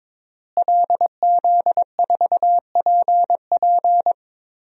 Morse code, heard as a keyed tone, about 22 wpm, 700 Hz. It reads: LZ4PP
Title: DXCC-100 Call Signs - 5 Characters - Repeated 3x - 22wpm